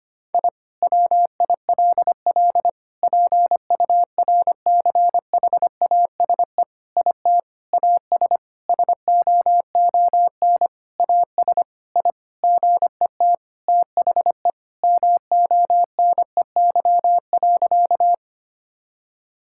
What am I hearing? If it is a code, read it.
Morse code, heard as a keyed tone, about 25 wpm, 700 Hz. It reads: I WILL PURCHASE IT AS SOON AS I GET THE MONEY.
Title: Sentences from Top 1000 Words - Review Part 5 - 25wpm